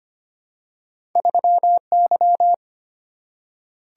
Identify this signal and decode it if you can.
Morse code, heard as a keyed tone, about 25 wpm, 700 Hz. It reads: 3Y
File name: Two Letter-Number Combinations - 25wpm